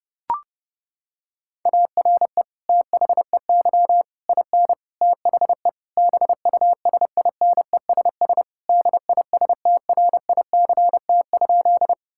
Morse code, heard as a keyed tone, about 30 wpm, 700 Hz. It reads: ARE THEY IN THE BUSINESS DISTRICT?
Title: Sentences from Top 1600 Words 30wpm